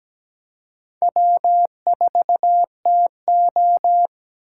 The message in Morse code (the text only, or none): W4TO